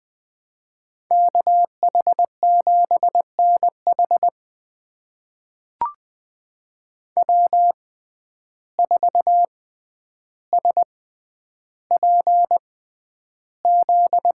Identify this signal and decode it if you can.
Morse code, heard as a keyed tone, about 20 wpm, spaced as for 10 wpm, 700 Hz. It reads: KH7NH W4SPZ
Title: USA Call Signs - 6x Character Spacing - 20wpm